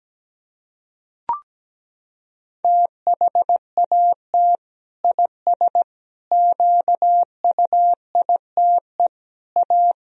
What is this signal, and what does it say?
Morse code, heard as a keyed tone, about 17 wpm, 700 Hz. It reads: THAT IS QUITE A
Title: Sentences from Top 1000 Words - Review Part 3 - 17wpm